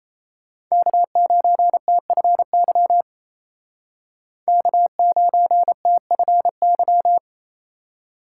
Morse code, heard as a keyed tone, about 35 wpm, 700 Hz. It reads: K9TFY K9TFY